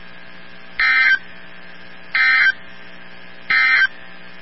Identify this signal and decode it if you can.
Emergency Alert System (SAME) tones, heard as harsh buzzing bursts. It is heard from 0.7 to 4.0 s.
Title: NOAA Weather Radio 162.550